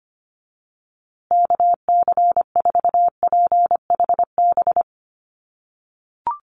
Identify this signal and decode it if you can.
Morse code, heard as a keyed tone, about 25 wpm, 700 Hz. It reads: KC4PHB
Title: USA Call Signs Repeated 2x 25wpm